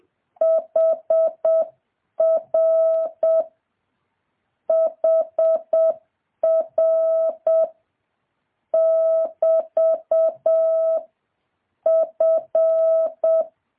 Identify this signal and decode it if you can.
Morse code, heard as a keyed tone, about 7 wpm, 650 Hz. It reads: HR HR <BT> F
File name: jh8cbh / 2023cw